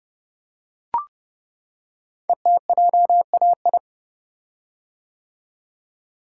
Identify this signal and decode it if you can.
Morse code, heard as a keyed tone, about 30 wpm, 700 Hz. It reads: ETJAI
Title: Code Groups - 5 Letters - 30wpm